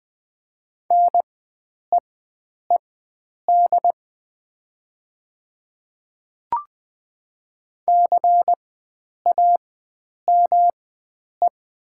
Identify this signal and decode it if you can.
Morse code, heard as a keyed tone, about 20 wpm, spaced as for 9 wpm, 700 Hz. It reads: NEED CAME
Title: 4x Character Spacing - Top 300 Words - 20wpm